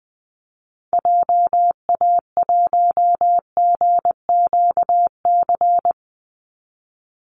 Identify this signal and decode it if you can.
Morse code, heard as a keyed tone, about 20 wpm, 700 Hz. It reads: JA1GQC
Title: Call Signs - 20wpm